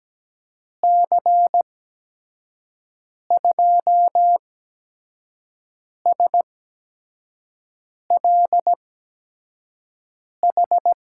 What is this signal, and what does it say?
Morse code, heard as a keyed tone, about 17 wpm, spaced as for 5 wpm, 700 Hz. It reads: C2SLH
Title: USA Call Signs - 8x Character Spacing - 17wpm